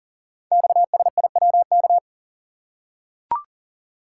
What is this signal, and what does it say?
Morse code, heard as a keyed tone, about 38 wpm, 700 Hz. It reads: XSIWK
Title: Code Groups - 5 Letters - 40wpm